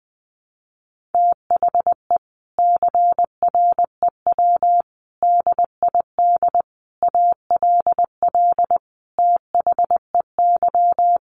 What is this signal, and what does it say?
Morse code, heard as a keyed tone, about 20 wpm, 700 Hz. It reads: THE CREW DID ALL THEY
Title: Sentences from Top 1500 Words 20wpm